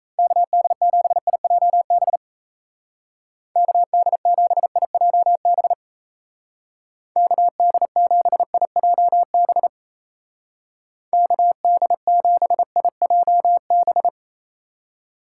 Morse code, heard as a keyed tone, about 34 wpm, 700 Hz. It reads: KD7IJB KD7IJB KD7IJB KD7IJB
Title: Speed Racing - USA 2x3 Call Signs 28wpm